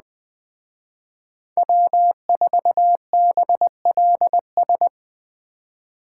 Morse code, heard as a keyed tone, about 21 wpm, 700 Hz. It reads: W4BLS
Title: Speed Racing - USA 1x3 Call Signs 17wpm